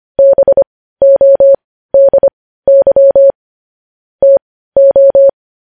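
Morse code, heard as a keyed tone, about 25 wpm, spaced as for 15 wpm, 550 Hz. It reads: BODY TO
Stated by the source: Morse code news headlines sent at 15, opening with vvv.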